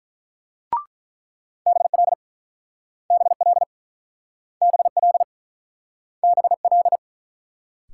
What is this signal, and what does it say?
Morse code, heard as a keyed tone, about 44 wpm, 700 Hz. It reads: BL BL BL BL